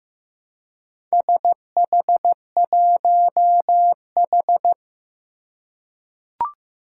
Morse code, heard as a keyed tone, about 15 wpm, 700 Hz. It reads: SH1H
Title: Four Challenging Letter-Number Combinations 15wpm